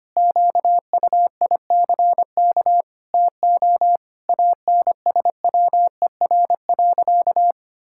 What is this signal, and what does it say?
Morse code, heard as a keyed tone, about 25 wpm, 700 Hz. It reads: QUICK TO ANSWER.
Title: Sentences from Top 1000 Words - Review Part 6 - 25wpm